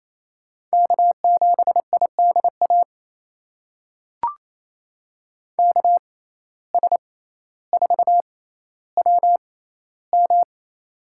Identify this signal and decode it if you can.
Morse code, heard as a keyed tone, about 27 wpm, spaced as for 14 wpm, 700 Hz. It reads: K7IDA KS4WM
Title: USA Call Signs - 5 Characters - 6x Character Spacing - 28wpm